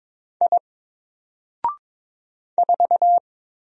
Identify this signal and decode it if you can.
Morse code, heard as a keyed tone, about 22 wpm, 700 Hz. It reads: I 4
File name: Single-Letter-Number-22wpm